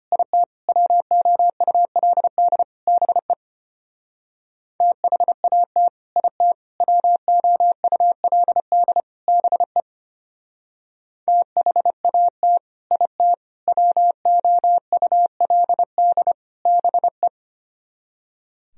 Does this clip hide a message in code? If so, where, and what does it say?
Yes: Morse code from 0.1 to 17.3 s, heard as a keyed tone, about 33 wpm, 700 Hz. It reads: IT WOULD BE THAT IT WOULD BE THAT IT WOULD BE